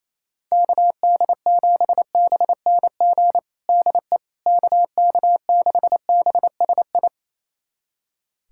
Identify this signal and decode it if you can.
Morse code, heard as a keyed tone, about 28 wpm, 700 Hz. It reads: KD7BNG DE KK6BSI
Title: USA Callsign DE Callsign - 6 Characters - 28wpm